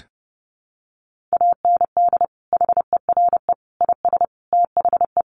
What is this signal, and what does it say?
Morse code, heard as a keyed tone, about 28 wpm, 700 Hz. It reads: AND HERE IS THE
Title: Sentences from Top 500 Words 30wpm